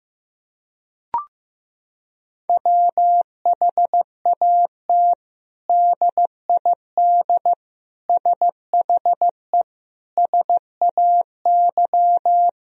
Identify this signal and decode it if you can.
Morse code, heard as a keyed tone, about 15 wpm, 700 Hz. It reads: WHAT DID SHE SAY
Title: Sentences from Top 1000 Words - Review Part 2 - 15wpm